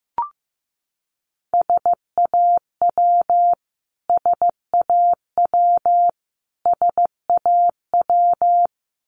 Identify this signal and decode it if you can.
Morse code, heard as a keyed tone, about 15 wpm, 700 Hz. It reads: SAW SAW SAW
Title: Top 300 Words Repeated 15wpm